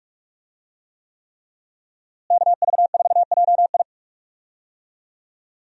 Morse code, heard as a keyed tone, about 47 wpm, 700 Hz. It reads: KV4JI
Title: USA Call Signs - 5 Characters - Repeated 3x - 45wpm